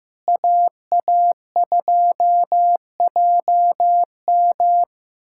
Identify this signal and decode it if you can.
Morse code, heard as a keyed tone, about 15 wpm, 700 Hz. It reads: AA2JM
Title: USA Callsign DE Callsign - 5 Characters - 15wpm